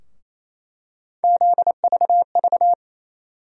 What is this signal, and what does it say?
Morse code, heard as a keyed tone, about 28 wpm, 700 Hz. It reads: ZVV
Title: USA Call Sign Postfixes - 3 Letters 28wpm